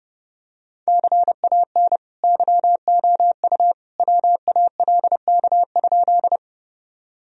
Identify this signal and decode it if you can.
Morse code, heard as a keyed tone, about 30 wpm, 700 Hz. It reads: CAN YOU WALK?
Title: Sentences from Top 1000 Words - Review Part 1 - 30wpm